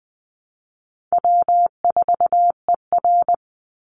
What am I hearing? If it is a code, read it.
Morse code, heard as a keyed tone, about 20 wpm, 700 Hz. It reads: W4ER
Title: Call Signs Repeated 3x - 20wpm